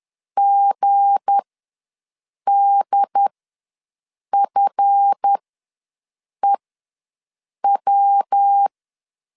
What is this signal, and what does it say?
Morse code, heard as a keyed tone, about 10 wpm, spaced as for 6 wpm, 800 Hz. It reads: GDFEW
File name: LEZIONI / 7 GF9Z / 7 GF ws 10wpm 3918